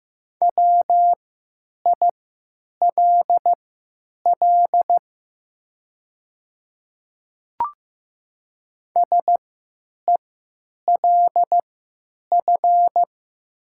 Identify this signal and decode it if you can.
Morse code, heard as a keyed tone, about 15 wpm, spaced as for 9 wpm, 700 Hz. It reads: WILL SELF
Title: 3x Character Spacing - Top 200 Words - 15wpm